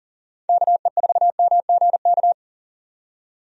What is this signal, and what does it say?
Morse code, heard as a keyed tone, about 40 wpm, 700 Hz. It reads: KE4MGK
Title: USA Call Signs - 6 Characters - 40wpm